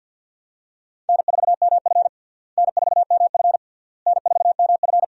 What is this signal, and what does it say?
Morse code, heard as a keyed tone, about 51 wpm, 700 Hz. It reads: N4MF N4MF N4MF
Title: USA Call Signs - 4 Characters - Repeated 3x - 50wpm